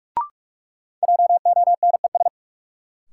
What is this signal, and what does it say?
Morse code, heard as a keyed tone, about 46 wpm, 700 Hz. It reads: JONES
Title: Sets of 5 Words - New Word Drill - 45wpm